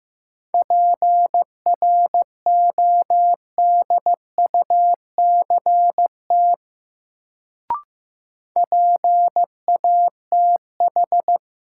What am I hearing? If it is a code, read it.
Morse code, heard as a keyed tone, about 15 wpm, 700 Hz. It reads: PRODUCT PATH
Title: Sentences from Top 1400 Words - New Word Drill Alt - 15wpm